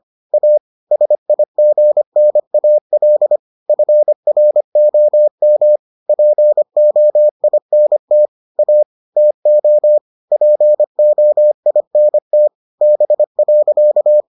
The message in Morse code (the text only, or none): A SIGNAL FROM POINT A TO POINT B.